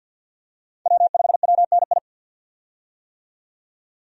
Morse code, heard as a keyed tone, about 47 wpm, 700 Hz. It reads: W5WNI
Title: USA Call Signs - 5 Characters - 50wpm